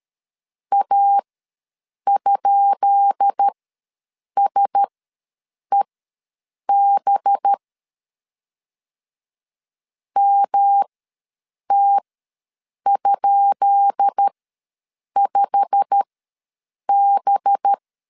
Morse code, heard as a keyed tone, about 13 wpm, spaced as for 7 wpm, 800 Hz. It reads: A?SEB MT?5B
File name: LEZIONI / 10 SIGNS / 10B INT ws 12wpm 3527